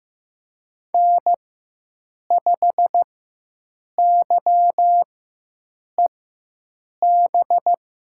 Morse code, heard as a keyed tone, about 15 wpm, spaced as for 7 wpm, 700 Hz. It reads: N5YEB
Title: USA Call Signs - 5 Characters - 4x Character Spacing - 15wpm